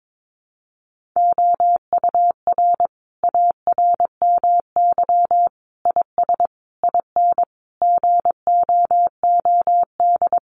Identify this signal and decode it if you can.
Morse code, heard as a keyed tone, about 22 wpm, 700 Hz. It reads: OUR ARMY IS IN GOOD
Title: Sentences from Top 1500 Words 22wpm